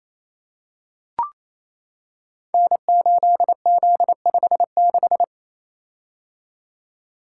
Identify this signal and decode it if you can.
Morse code, heard as a keyed tone, about 27 wpm, 700 Hz. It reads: N8Z56
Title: Code Groups - 5 Letter:Numbers - 28wpm